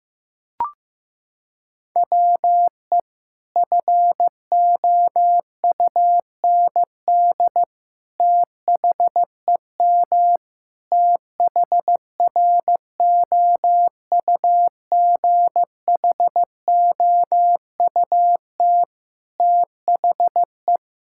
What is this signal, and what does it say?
Morse code, heard as a keyed tone, about 15 wpm, 700 Hz. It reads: WE FOUND THEM THROUGHOUT THE
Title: Sentences from Top 1000 Words - Review Part 5 - 15wpm